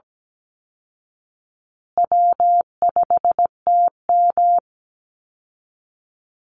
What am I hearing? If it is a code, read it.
Morse code, heard as a keyed tone, about 17 wpm, 700 Hz. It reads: W5TM
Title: Easy Call Signs Repeated 2x - 17wpm